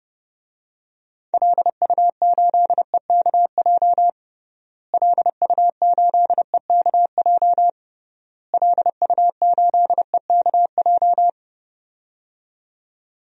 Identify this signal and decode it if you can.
Morse code, heard as a keyed tone, about 32 wpm, 700 Hz. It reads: LU8EKJ LU8EKJ LU8EKJ
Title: DXCC-100 Call Signs - 6 Characters - Repeated 3x - 30wpm